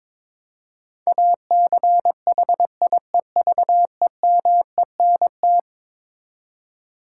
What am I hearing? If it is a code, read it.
Morse code, heard as a keyed tone, about 22 wpm, 700 Hz. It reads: ACHIEVEMENT
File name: practice / 22wpm / Eleven Letter Words - 22wpm